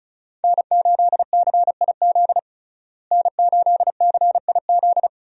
Morse code, heard as a keyed tone, about 35 wpm, 700 Hz. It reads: N8CIZ N8CIZ
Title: USA Call Signs - 5 Characters - Repeated 2x - 35wpm